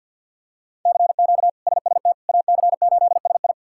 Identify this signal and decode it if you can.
Morse code, heard as a keyed tone, about 53 wpm, 700 Hz. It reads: CQ SST AC8SI
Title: K1USN SST QSO Practice - 50wpm